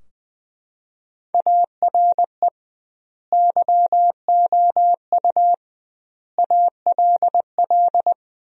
Morse code, heard as a keyed tone, about 20 wpm, 700 Hz. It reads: ARE YOU ALL
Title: 2x Word-Spacing - Sentences from Top 400 Words - 20wpm